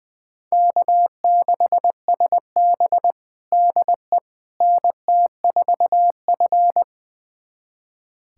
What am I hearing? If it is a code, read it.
Morse code, heard as a keyed tone, about 20 wpm, 700 Hz. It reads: K6SB DE NT4F
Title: USA Callsign DE Callsign - 4 Characters - 20wpm